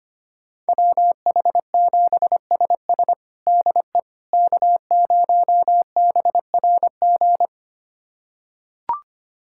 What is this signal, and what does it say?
Morse code, heard as a keyed tone, about 25 wpm, 700 Hz. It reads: WH7SS DE K0BRG
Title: USA Callsign DE Callsign - 5 Characters - 25wpm